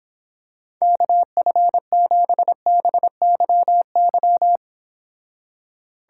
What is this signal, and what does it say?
Morse code, heard as a keyed tone, about 26 wpm, 700 Hz. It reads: KF7BYY